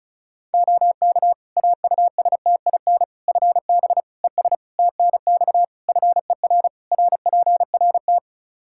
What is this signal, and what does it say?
Morse code, heard as a keyed tone, about 36 wpm, 700 Hz. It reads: OK AUSTIN FB ES TNX FER RPRT
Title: Practice QSO - Exchange 2 - Info - 35wpm